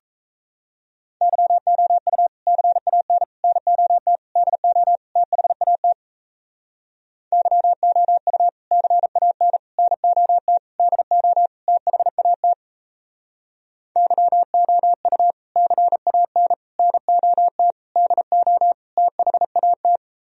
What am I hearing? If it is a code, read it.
Morse code, heard as a keyed tone, about 36 wpm, 700 Hz. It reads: YOU CAN NOT DO THAT YOU CAN NOT DO THAT YOU CAN NOT DO THAT